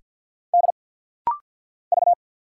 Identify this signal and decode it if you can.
Morse code, heard as a keyed tone, about 50 wpm, 700 Hz. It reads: D V
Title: Single Letter-Number Alt - Rapid-Fire 50wpm